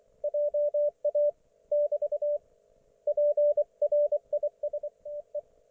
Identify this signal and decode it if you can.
Morse code, heard as a keyed tone, about 24 wpm, 590 Hz. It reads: JA <BT> PRISTE
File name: Archiv / CUC212 / ok1fhi cul212